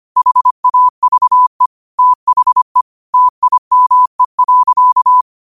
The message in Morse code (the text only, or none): HAVE THE TIME.